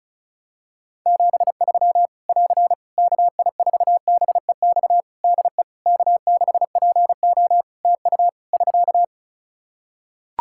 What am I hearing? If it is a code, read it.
Morse code, heard as a keyed tone, about 34 wpm, 700 Hz. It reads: 73 <AR> KI4BEX DE K6PO TU <SK>
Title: Practice QSO - Exchange 4 - Ending - 35wpm